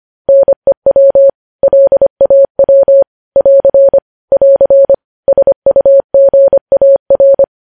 Morse code, heard as a keyed tone, about 25 wpm, 550 Hz. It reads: NEW LAW <AR> <AR> SUGAR